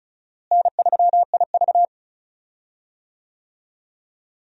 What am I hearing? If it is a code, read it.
Morse code, heard as a keyed tone, about 34 wpm, 700 Hz. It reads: N3IV
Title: USA Call Signs - 4 Characters - Repeated 2x - 35wpm